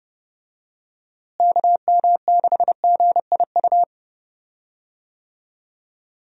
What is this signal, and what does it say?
Morse code, heard as a keyed tone, about 30 wpm, 700 Hz. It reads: KM6GIU
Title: USA Call Signs - 6 Characters - Repeated 2x - 30wpm